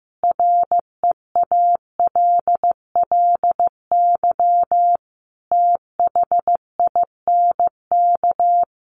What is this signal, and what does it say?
Morse code, heard as a keyed tone, about 15 wpm, 700 Hz. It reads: REALLY THINK
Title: Sentences from Top 600 Words 15wpm